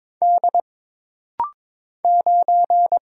Morse code, heard as a keyed tone, about 22 wpm, 700 Hz. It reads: D 9
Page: This rapid-fire timing is geared to push your ICR (Instant Character Recognition) skills.